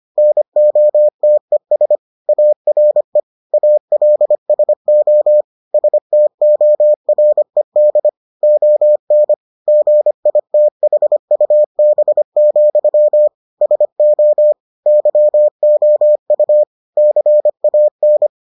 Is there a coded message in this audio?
Yes: Morse code from 0.2 to 18.3 s, heard as a keyed tone, about 25 wpm, 600 Hz. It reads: NOTES ARE ALSO STORED ON GITHUB, SO YOU CAN